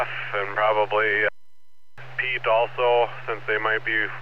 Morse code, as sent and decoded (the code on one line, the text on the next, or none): none